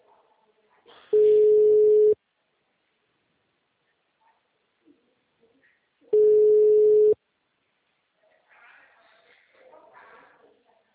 Ringback tone (Europe, 425 Hz).
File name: out- -2014- - - .50976